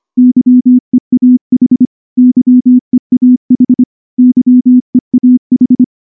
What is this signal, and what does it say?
Morse code, heard as a keyed tone, about 25 wpm, 250 Hz. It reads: YEAH YEAH YEAH
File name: morse / morse-training-25WPM